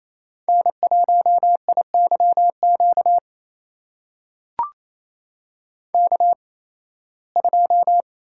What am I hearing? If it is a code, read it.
Morse code, heard as a keyed tone, about 28 wpm, spaced as for 18 wpm, 700 Hz. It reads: N1IYQ K2
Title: USA Call Signs - 5 Characters - 8x Character Spacing - 28wpm